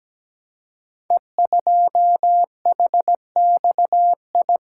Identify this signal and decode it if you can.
Morse code, heard as a keyed tone, about 17 wpm, 700 Hz. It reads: E2HXI
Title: Code Groups - 5 Letter:Numbers - 17wpm